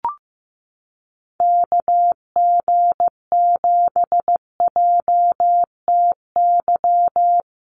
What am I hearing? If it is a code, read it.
Morse code, heard as a keyed tone, about 15 wpm, 700 Hz. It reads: KG7JTY
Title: USA Call Signs Repeated 2x 15wpm